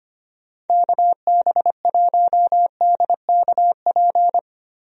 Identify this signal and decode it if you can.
Morse code, heard as a keyed tone, about 25 wpm, 700 Hz. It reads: KB1DKP
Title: USA Call Signs - 6 Characters - 25wpm